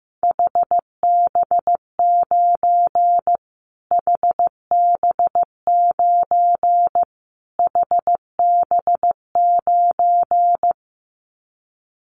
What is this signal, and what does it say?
Morse code, heard as a keyed tone, about 14 wpm, 700 Hz. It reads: HB9 HB9 HB9
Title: Callsign Prefixes for Top 25 DXCC - Repeated 3x - 15wpm